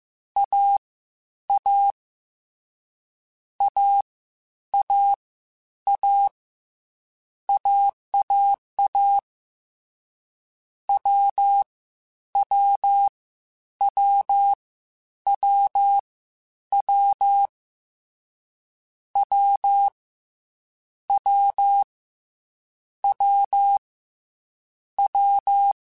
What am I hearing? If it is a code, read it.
Morse code, heard as a keyed tone, about 15 wpm, 800 Hz. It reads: A A A A A AAA W W W W W W W W W
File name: Difficult Combos / Difficult Combos-mp3s / AWJ1 @13